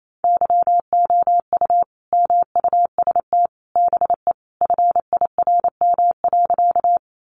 Morse code, heard as a keyed tone, about 28 wpm, 700 Hz. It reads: YOU MUST BE FIRM.
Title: Sentences from Top 1400 Words 28wpm